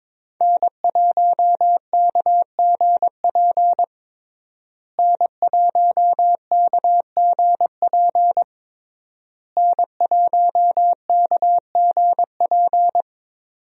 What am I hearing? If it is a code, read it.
Morse code, heard as a keyed tone, about 22 wpm, 700 Hz. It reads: N1KGP N1KGP N1KGP
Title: USA Call Signs - 5 Characters - Repeated 3x - 22wpm